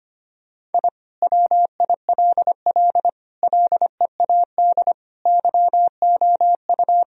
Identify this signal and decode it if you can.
Morse code, heard as a keyed tone, about 25 wpm, 700 Hz. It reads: I WILL LEAD YOU
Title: Sentences from Top 1000 Words - Review Part 2 - 25wpm